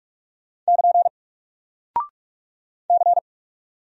Morse code, heard as a keyed tone, about 44 wpm, 700 Hz. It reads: <KN> C
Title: Single Character (Letter:Number:Punctuation:Prosign) Alt 45wpm